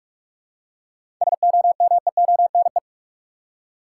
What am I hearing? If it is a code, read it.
Morse code, heard as a keyed tone, about 46 wpm, 700 Hz. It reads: SOMEONE
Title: Seven Letter Words - 45wpm